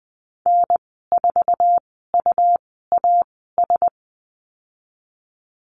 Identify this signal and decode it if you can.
Morse code, heard as a keyed tone, about 20 wpm, spaced as for 15 wpm, 700 Hz. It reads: N4UAS
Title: 147 Lesson 24g - Letter y - Call Signs 20wpm